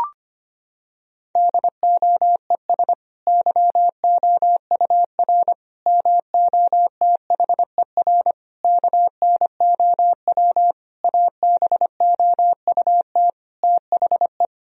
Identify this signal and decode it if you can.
Morse code, heard as a keyed tone, about 25 wpm, 700 Hz. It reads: DOES YOUR MOTHER KNOW ABOUT THE
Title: Sentences from Top 1000 Words - Review Part 4 - 25wpm